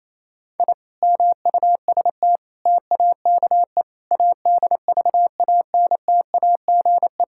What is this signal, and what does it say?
Morse code, heard as a keyed tone, about 28 wpm, 700 Hz. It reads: I MUST TAKE ADVANTAGE
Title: Sentences from Top 1000 Words - Review Part 6 - 28wpm